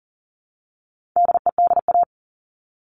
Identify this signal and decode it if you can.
Morse code, heard as a keyed tone, about 40 wpm, 700 Hz. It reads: DEDA
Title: All Cut Numbers - 4-Digit 40wpm